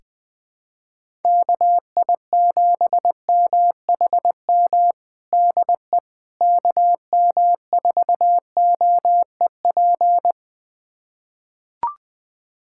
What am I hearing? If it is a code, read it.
Morse code, heard as a keyed tone, about 21 wpm, 700 Hz. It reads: KI7MHM DE KM4OEP
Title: USA Callsign DE Callsign - 6 Characters - 20wpm